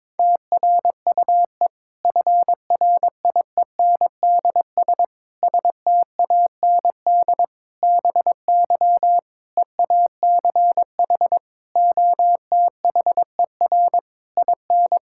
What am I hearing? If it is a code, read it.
Morse code, heard as a keyed tone, about 22 wpm, 700 Hz. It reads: TRUE FRIENDS STAND BY EACH OTHER IN
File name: Difficult Sentences with Hints - 22wpm